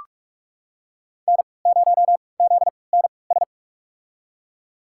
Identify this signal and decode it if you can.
Morse code, heard as a keyed tone, about 46 wpm, spaced as for 25 wpm, 700 Hz. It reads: N0ZNS
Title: USA Call Signs - 5 Characters - 3x Character Spacing - 45wpm